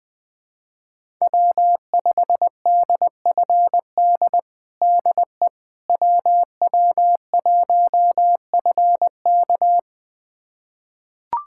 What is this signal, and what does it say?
Morse code, heard as a keyed tone, about 20 wpm, 700 Hz. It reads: W5DFD DE WW1FK
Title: USA Callsign DE Callsign - 5 Characters - 20wpm